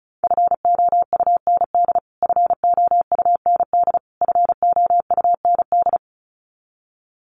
Morse code, heard as a keyed tone, about 34 wpm, 700 Hz. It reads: FOUND FOUND FOUND
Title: Top 500 Words Repeated 35wpm